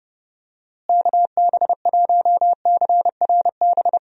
Morse code, heard as a keyed tone, about 30 wpm, 700 Hz. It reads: KB1CRB
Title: USA Call Signs - 6 Characters - Repeated 2x - 30wpm